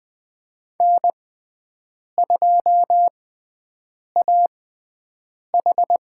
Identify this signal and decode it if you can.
Morse code, heard as a keyed tone, about 20 wpm, spaced as for 7 wpm, 700 Hz. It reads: N2AH
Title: USA Call Signs - 6x Character Spacing - 20wpm